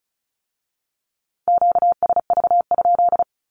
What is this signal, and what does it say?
Morse code, heard as a keyed tone, about 36 wpm, 700 Hz. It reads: QSV?
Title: Q-Codes with Spoken Meaning - 35wpm